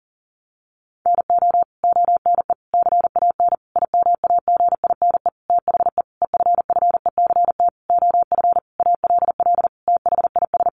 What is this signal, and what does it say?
Morse code, heard as a keyed tone, about 40 wpm, 700 Hz. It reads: NO ONE CAN IMAGINE THE EFFECT OF ALL THIS
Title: Sentences from Top 1400 Words 40wpm